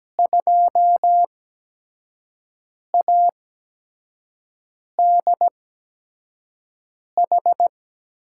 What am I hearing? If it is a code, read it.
Morse code, heard as a keyed tone, about 17 wpm, spaced as for 5 wpm, 700 Hz. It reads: 2ADH
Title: USA Call Signs - 6 Characters - 8x Character Spacing - 17wpm